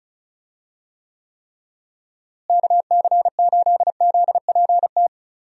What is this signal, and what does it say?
Morse code, heard as a keyed tone, about 36 wpm, 700 Hz. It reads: KC8ZPT
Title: USA Call Signs - 6 Characters - Repeated 3x - 35wpm